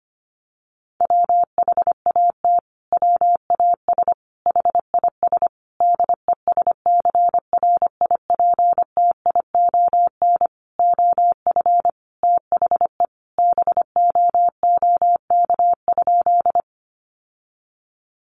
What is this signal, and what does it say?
Morse code, heard as a keyed tone, about 25 wpm, 700 Hz. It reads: WHAT WAS HIS DESCRIPTION OF THE BOOK?
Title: Sentences from Top 1300 Words 25wpm